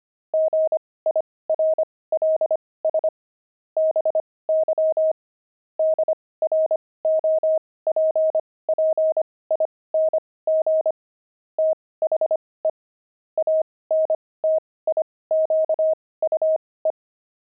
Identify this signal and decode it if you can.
Morse code, heard as a keyed tone, about 25 wpm, spaced as for 18 wpm, 630 Hz. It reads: GIRLS BY DROPPING THE ANTIQUE
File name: pangram-fs-18-cs-25-antique